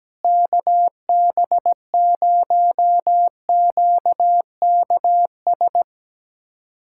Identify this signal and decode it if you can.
Morse code, heard as a keyed tone, about 17 wpm, 700 Hz. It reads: KB0QKS
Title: USA Callsign DE Callsign 17wpm